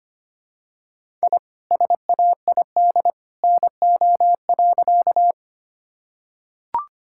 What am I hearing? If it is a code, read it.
Morse code, heard as a keyed tone, about 25 wpm, 700 Hz. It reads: I SAID NO.
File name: practice / 25wpm / Sentences with 2-Word Phrases - 25wpm